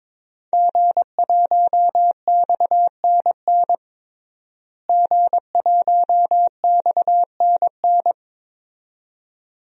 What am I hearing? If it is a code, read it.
Morse code, heard as a keyed tone, about 22 wpm, 700 Hz. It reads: G1XNN G1XNN
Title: DXCC-100 Call Signs - 5 Characters - Repeated 2x - 22wpm